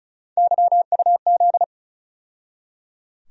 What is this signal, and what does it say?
Morse code, heard as a keyed tone, about 36 wpm, 700 Hz. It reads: YUZ
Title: USA Call Sign Postfixes - 3 Letters 35wpm